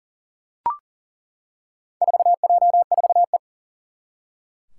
Morse code, heard as a keyed tone, about 40 wpm, 690 Hz. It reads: 4J4E
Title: Four Challenging Letter-Number Combinations 40wpm